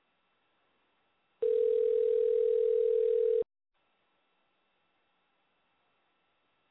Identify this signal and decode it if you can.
Ringback tone (North America), 440 + 480 Hz.